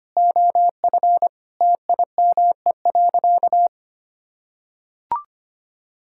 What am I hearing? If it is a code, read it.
Morse code, heard as a keyed tone, about 25 wpm, 700 Hz. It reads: OF TIME.
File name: practice / 25wpm / Sentences with 4-Word Phrases - 25wpm